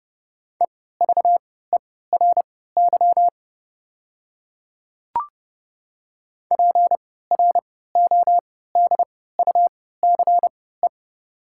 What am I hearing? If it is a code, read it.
Morse code, heard as a keyed tone, about 29 wpm, spaced as for 17 wpm, 700 Hz. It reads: EVERY PRODUCE
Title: 3x Character Spacing - Top 500 Words - 30wpm